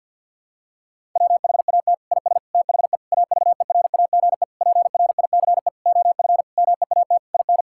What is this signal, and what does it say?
Morse code, heard as a keyed tone, about 50 wpm, 700 Hz. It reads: WHAT IS THE AVERAGE PRICE OF MEAT IN
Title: Sentences from Top 1000 Words - Review Part 6 - 50wpm